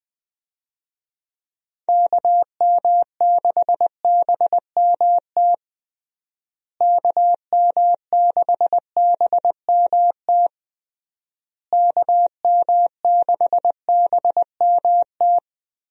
Morse code, heard as a keyed tone, about 21 wpm, 700 Hz. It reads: KM6BMT KM6BMT KM6BMT
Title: USA Call Signs - 6 Characters - Repeated 3x - 20wpm